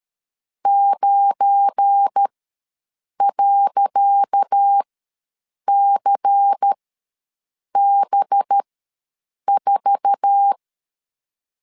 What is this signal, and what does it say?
Morse code, heard as a keyed tone, about 13 wpm, spaced as for 7 wpm, 800 Hz. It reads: 9.CB4
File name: LEZIONI / 10 SIGNS / 10C DOT ws 12wpm 3548